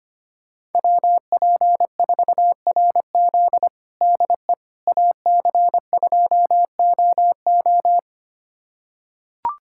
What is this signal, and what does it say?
Morse code, heard as a keyed tone, about 25 wpm, 700 Hz. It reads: WP4RZ DE AC2OO
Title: USA Callsign DE Callsign - 5 Characters - 25wpm